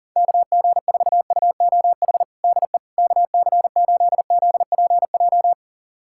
Morse code, heard as a keyed tone, about 40 wpm, 700 Hz. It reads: KG4UOH DE KC8ZPJ
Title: USA Callsign DE Callsign - 6 Characters - 40wpm